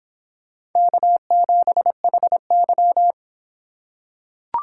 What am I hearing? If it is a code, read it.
Morse code, heard as a keyed tone, about 25 wpm, 700 Hz. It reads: K7HY